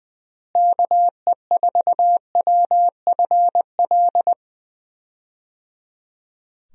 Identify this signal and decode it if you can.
Morse code, heard as a keyed tone, about 20 wpm, 700 Hz. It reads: KE4WFL
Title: USA Call Signs - 6 Characters - Repeated 3x - 20wpm